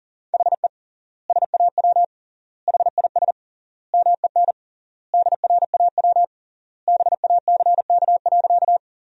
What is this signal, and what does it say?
Morse code, heard as a keyed tone, about 40 wpm, 700 Hz. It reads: HE SAW HIS MEN DRAW BACK.
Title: 3x Word-Spacing - Sentences from Top 300 Words - 40wpm